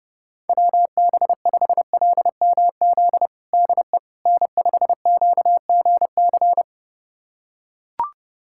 Morse code, heard as a keyed tone, about 30 wpm, 700 Hz. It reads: WB5LMZ DE N5QGC
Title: USA Callsign DE Callsign 30wpm